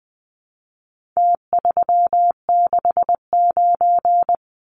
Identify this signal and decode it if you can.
Morse code, heard as a keyed tone, about 20 wpm, 700 Hz. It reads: T369
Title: Numbers - Contest Style - 4-Digit 20wpm